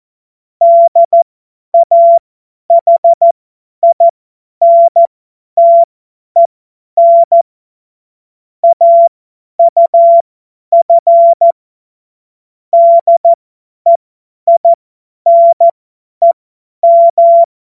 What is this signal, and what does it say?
Morse code, heard as a keyed tone, about 14 wpm, spaced as for 10 wpm, 680 Hz. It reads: DAHINTEN AUF DEINEM
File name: Witz-09-September-2025-50-F70